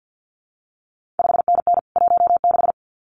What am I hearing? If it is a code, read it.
Morse code, heard as a keyed tone, about 50 wpm, 700 Hz. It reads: 5NN 16
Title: CQ WW DX Contest Exchange - 50wpm